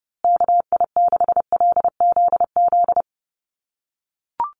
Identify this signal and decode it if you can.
Morse code, heard as a keyed tone, about 30 wpm, 700 Hz. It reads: KI6LZZ
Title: USA Call Signs Repeated 3x 30wpm